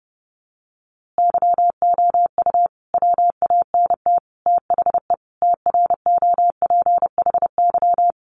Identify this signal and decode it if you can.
Morse code, heard as a keyed tone, about 30 wpm, 700 Hz. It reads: YOU WANT THE TROPHY
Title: A Sound Of Thunder 30wpm